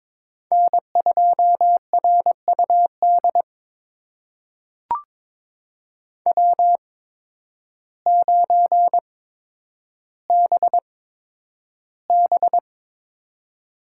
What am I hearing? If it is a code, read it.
Morse code, heard as a keyed tone, about 22 wpm, spaced as for 10 wpm, 700 Hz. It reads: N2RUD W9BB
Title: USA Call Signs - 5 Characters - 8x Character Spacing - 22wpm